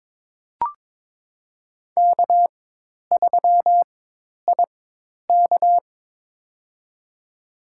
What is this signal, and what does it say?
Morse code, heard as a keyed tone, about 22 wpm, spaced as for 10 wpm, 700 Hz. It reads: K3IK
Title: USA Call Signs - 4 Characters - 4x Character Spacing - 22wpm